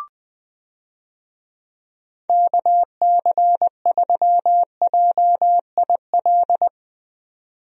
Morse code, heard as a keyed tone, about 20 wpm, 700 Hz. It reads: KC3JIL
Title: USA Call Signs - 6 Characters - 20wpm